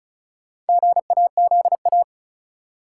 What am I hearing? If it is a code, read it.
Morse code, heard as a keyed tone, about 34 wpm, 700 Hz. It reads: GAZA
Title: Sets of 5 Words - New Word Drill - 35wpm